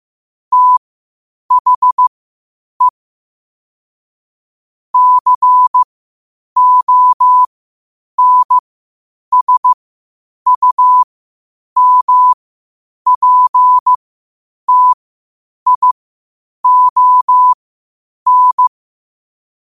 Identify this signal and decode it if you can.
Morse code, heard as a keyed tone, about 15 wpm, spaced as for 8 wpm, 1000 Hz. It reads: THE CONSUMPTION
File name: qotdfiles / 8wpm